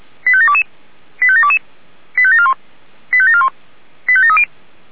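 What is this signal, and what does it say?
Five-tone selective call (ZVEI), heard as quick runs of five tones, tones of about 70 ms. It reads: 85722 85722 85741 85741 85730
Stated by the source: ZVEI five-tone selective call 85825, received by SDR.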